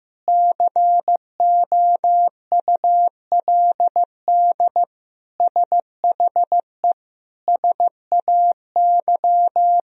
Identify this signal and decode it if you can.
Morse code, heard as a keyed tone, about 15 wpm, 700 Hz. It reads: COULD SHE SAY
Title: Sentences from Top 200 Words and Abbreviations 15wpm